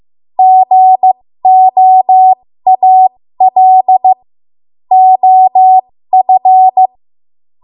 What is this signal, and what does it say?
Morse code, heard as a keyed tone, about 15 wpm, spaced as for 13 wpm, 750 Hz. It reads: GOAL OF
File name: Morse / 13WPM